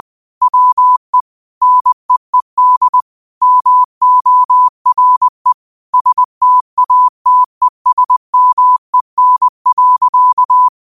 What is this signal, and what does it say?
Morse code, heard as a keyed tone, about 20 wpm, 1000 Hz. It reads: WE NEED MORE STATESMEN.
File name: qotdfiles / 20wpm